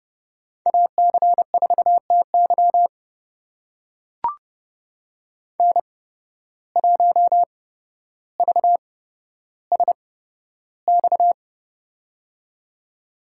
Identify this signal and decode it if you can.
Morse code, heard as a keyed tone, about 30 wpm, spaced as for 13 wpm, 700 Hz. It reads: AC4TY N1VSX
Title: USA Call Signs - 5 Characters - 8x Character Spacing - 30wpm